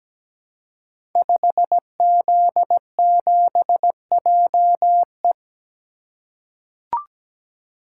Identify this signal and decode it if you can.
Morse code, heard as a keyed tone, about 17 wpm, 690 Hz. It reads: 5Z7JE
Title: Code Groups - 5 Letter:Numbers - 17wpm